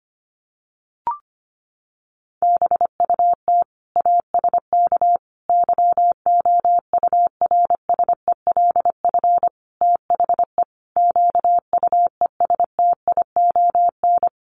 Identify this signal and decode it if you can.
Morse code, heard as a keyed tone, about 25 wpm, 700 Hz. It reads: BUT ASK YOURSELF THE QUESTION
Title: Sentences from Top 600 Words 25wpm